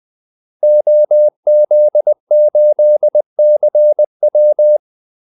Morse code, heard as a keyed tone, about 20 wpm, 600 Hz. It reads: OZ8CW
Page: Kaldesignaler 20 wpm